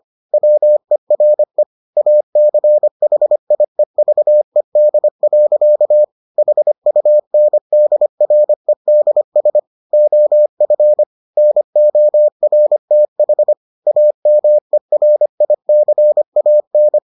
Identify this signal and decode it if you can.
Morse code, heard as a keyed tone, about 25 wpm, 600 Hz. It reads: WERE ACHIEVED. HUNDREDS OF NORTH AMERICAN